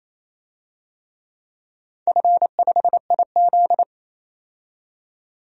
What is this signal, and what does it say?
Morse code, heard as a keyed tone, about 28 wpm, 700 Hz. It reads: F5IZ
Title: DXCC-100 Call Signs - 4 Characters - 28wpm